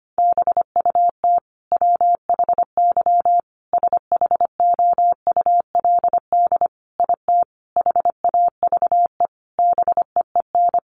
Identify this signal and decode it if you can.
Morse code, heard as a keyed tone, about 25 wpm, 700 Hz. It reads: BUT WHY SHOULD IT HAVE BEEN
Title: Sentences from Top 1200 Words 25wpm